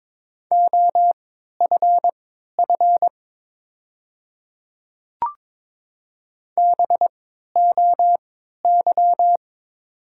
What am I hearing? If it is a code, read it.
Morse code, heard as a keyed tone, about 22 wpm, spaced as for 12 wpm, 700 Hz. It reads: OFF BOY
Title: 3x Character Spacing - Top 200 Words - 22wpm